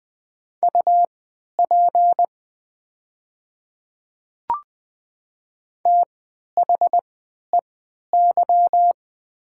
Morse code, heard as a keyed tone, about 20 wpm, spaced as for 11 wpm, 700 Hz. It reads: UP THEY
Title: 3x Character Spacing - Top 100 Words - 20wpm